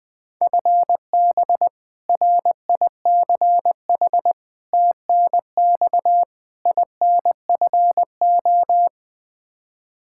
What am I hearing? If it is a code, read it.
Morse code, heard as a keyed tone, about 21 wpm, 700 Hz. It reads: FB RICH TNX INFO
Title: Practice QSO - Exchange 3 - More Info - 20wpm